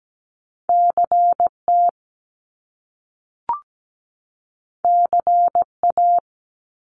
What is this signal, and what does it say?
Morse code, heard as a keyed tone, about 17 wpm, 700 Hz. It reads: CT CA
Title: US States 17wpm